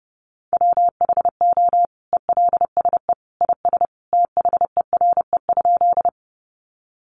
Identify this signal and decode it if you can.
Morse code, heard as a keyed tone, about 30 wpm, 700 Hz. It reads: WHO ELSE IS THERE?
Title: Sentences from Top 800 Words 30wpm